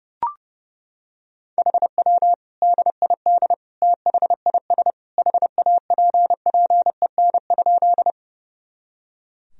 Morse code, heard as a keyed tone, about 30 wpm, 700 Hz. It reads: HW DID THIS HAPPEN?
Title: Sentences from Top 400 Words and Abbreviations 30wpm